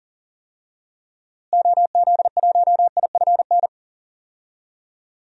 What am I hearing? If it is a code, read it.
Morse code, heard as a keyed tone, about 39 wpm, 700 Hz. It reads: OZ1IFN
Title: DXCC-100 Call Signs - 6 Characters - Repeated 3x - 40wpm